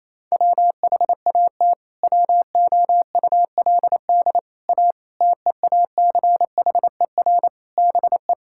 Morse code, heard as a keyed tone, about 28 wpm, 700 Hz. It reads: WHAT WOULD A TEACHER BE
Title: Sentences from Top 1000 Words - Review Part 6 - 28wpm